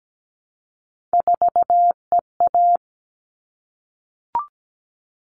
Morse code, heard as a keyed tone, about 17 wpm, 700 Hz. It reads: 4EA
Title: All Cut Numbers - 3-Digit 17wpm